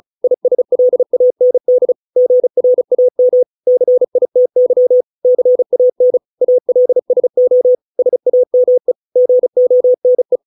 Morse code, heard as a keyed tone, about 34 wpm, 500 Hz. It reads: ISLAND GRAM CITY CAN ALSO SAME GONE